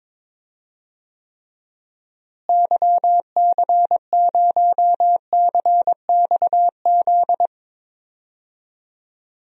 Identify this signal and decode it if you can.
Morse code, heard as a keyed tone, about 22 wpm, 700 Hz. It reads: YC0CXZ
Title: DXCC-100 Call Signs - 6 Characters - Repeated 2x - 22wpm